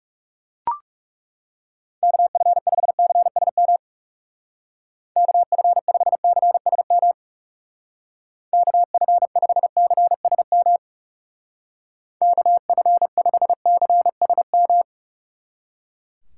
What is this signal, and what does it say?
Morse code, heard as a keyed tone, about 36 wpm, 700 Hz. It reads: KF5CSM KF5CSM KF5CSM KF5CSM
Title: Speed Racing - USA 2x3 Call Signs 30wpm